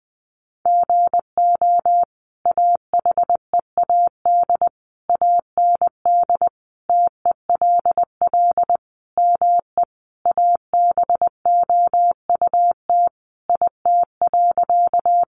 Morse code, heard as a keyed tone, about 20 wpm, 700 Hz. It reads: GO AHEAD AND TELL ME ABOUT IT.
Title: Sentences from Top 1100 Words 20wpm